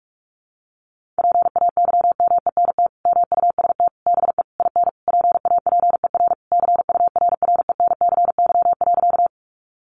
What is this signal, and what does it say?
Morse code, heard as a keyed tone, about 44 wpm, 700 Hz. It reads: PAYMENT MUST BE IN PAPER CURRENCY.
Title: Sentences from Top 1500 Words 45wpm